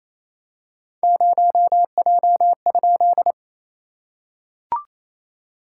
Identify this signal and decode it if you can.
Morse code, heard as a keyed tone, about 28 wpm, 700 Hz. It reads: 0J?
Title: SOTA QSO Practice 28wpm